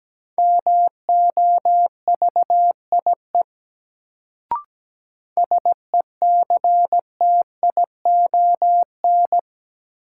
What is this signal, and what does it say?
Morse code, heard as a keyed tone, about 17 wpm, 700 Hz. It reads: MOVIE SECTION
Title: Sentences from Top 1000 Words - Review Part 5 - New Word Drill - 17wpm